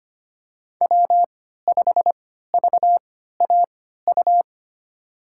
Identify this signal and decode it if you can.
Morse code, heard as a keyed tone, about 25 wpm, spaced as for 14 wpm, 700 Hz. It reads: W5VAU
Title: USA Call Signs - 5 Characters - 3x Character Spacing - 25wpm